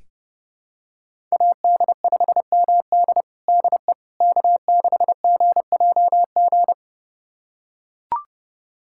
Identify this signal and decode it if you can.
Morse code, heard as a keyed tone, about 32 wpm, 700 Hz. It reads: AD5MD DE K6GJG
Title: USA Callsign DE Callsign - 5 Characters - 30wpm